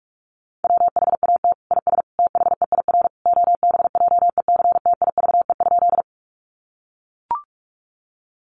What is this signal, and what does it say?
Morse code, heard as a keyed tone, about 44 wpm, 700 Hz. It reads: WHAT IS THEIR OBJECTIVE?
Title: Sentences from Top 1200 Words 45wpm